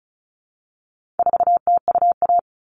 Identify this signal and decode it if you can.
Morse code, heard as a keyed tone, about 34 wpm, 700 Hz. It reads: 4TUA
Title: All Cut Numbers - 4-Digit 35wpm